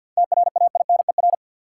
Morse code, heard as a keyed tone, about 53 wpm, 700 Hz. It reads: TRAINER
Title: Seven Letter Words - 50wpm